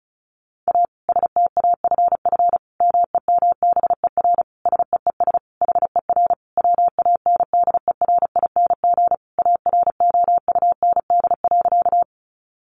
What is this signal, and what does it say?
Morse code, heard as a keyed tone, about 34 wpm, 700 Hz. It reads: A STAFF MEMBER SEES HER WANDERING AROUND.
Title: Additional Easy Phrases 35wpm